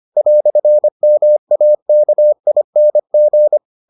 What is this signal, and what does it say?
Morse code, heard as a keyed tone, about 25 wpm, 600 Hz. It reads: "MAKING